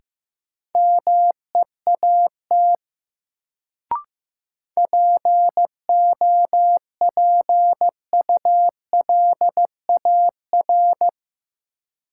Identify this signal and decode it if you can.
Morse code, heard as a keyed tone, about 15 wpm, 700 Hz. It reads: MEAT POPULAR
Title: Sentences from Top 1000 Words - Review Part 3 - New Word Drill - 15wpm